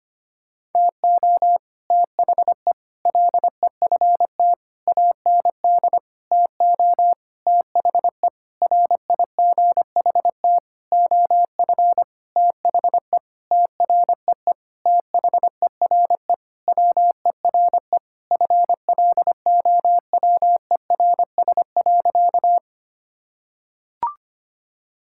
Morse code, heard as a keyed tone, about 25 wpm, 700 Hz. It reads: TO THE LEFT AND TO THE RIGHT OF THE TREE THERE WERE FLOWERS.
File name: practice / 25wpm / Sentences with 9-Word Phrases - 25wpm